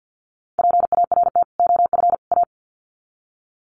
Morse code, heard as a keyed tone, about 50 wpm, 700 Hz. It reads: PART OF A
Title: Sets of 3 Words 50wpm